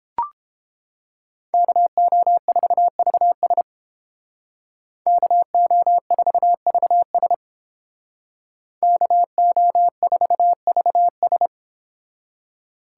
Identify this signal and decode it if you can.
Morse code, heard as a keyed tone, about 30 wpm, 700 Hz. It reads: KO4VS KO4VS KO4VS